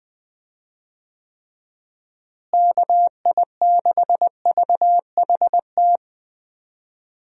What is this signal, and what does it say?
Morse code, heard as a keyed tone, about 20 wpm, 700 Hz. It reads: KI6VHT
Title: USA Call Signs - 6 Characters - Repeated 3x - 20wpm